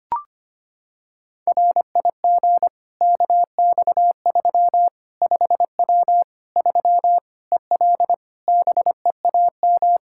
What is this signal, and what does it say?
Morse code, heard as a keyed tone, about 25 wpm, 700 Hz. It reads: RIG KX3 5W 3 EL BEAM
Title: K1USN SST QSO Practice 25wpm